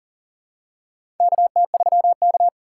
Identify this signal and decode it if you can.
Morse code, heard as a keyed tone, about 38 wpm, 700 Hz. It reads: KT3K
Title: USA Call Signs - 4 Characters - 4x Character Spacing - 40wpm